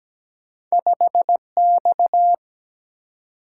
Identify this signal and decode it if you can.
Morse code, heard as a keyed tone, about 17 wpm, 700 Hz. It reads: 5X
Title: Single-Letter Single-Number 17wpm